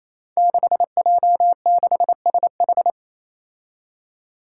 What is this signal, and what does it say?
Morse code, heard as a keyed tone, about 28 wpm, 700 Hz. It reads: 6J6SH
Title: Five Challenging Letter-Number Combinations 28wpm